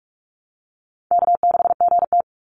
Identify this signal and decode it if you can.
Morse code, heard as a keyed tone, about 44 wpm, 700 Hz. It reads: K6GT
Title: Easy Call Signs Repeated 3x - 45wpm